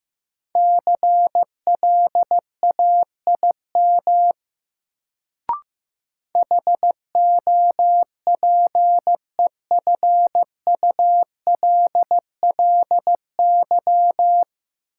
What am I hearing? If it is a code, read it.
Morse code, heard as a keyed tone, about 15 wpm, 700 Hz. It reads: CLAIM HOPEFULLY
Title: Sentences from Top 1500 Words - New Word Drill - 15wpm